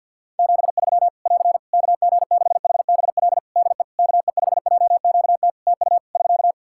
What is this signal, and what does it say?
Morse code, heard as a keyed tone, about 52 wpm, 700 Hz. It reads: 73 <AR> KG6HBL DE KE5JQT TU <SK>
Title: Practice QSO - Exchange 4 - Ending - 50wpm